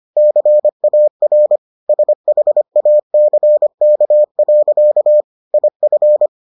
Morse code, heard as a keyed tone, about 25 wpm, 600 Hz. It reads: CAR SHACK. IF